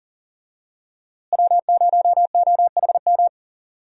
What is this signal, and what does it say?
Morse code, heard as a keyed tone, about 40 wpm, 700 Hz. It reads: W0OHM
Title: USA Call Signs - 5 Characters - Repeated 2x - 40wpm